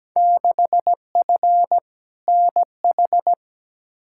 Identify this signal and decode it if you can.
Morse code, heard as a keyed tone, about 17 wpm, 700 Hz. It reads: 6F NH
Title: Field Day Exchanges Encore - 17wpm